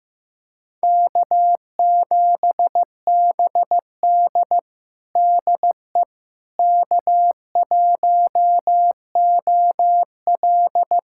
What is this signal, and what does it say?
Morse code, heard as a keyed tone, about 15 wpm, 700 Hz. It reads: K7BD DE K1OL
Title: USA Callsign DE Callsign - 4 Characters - 15wpm